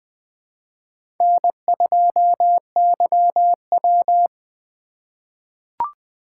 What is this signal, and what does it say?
Morse code, heard as a keyed tone, about 20 wpm, 700 Hz. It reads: N2YW
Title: USA Call Signs - 4 Characters - 3x Character Spacing - 20wpm